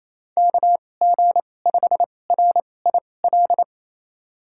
Morse code, heard as a keyed tone, about 28 wpm, spaced as for 20 wpm, 700 Hz. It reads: KG5RIL
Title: USA Call Signs - 2x Character Spacing - 28wpm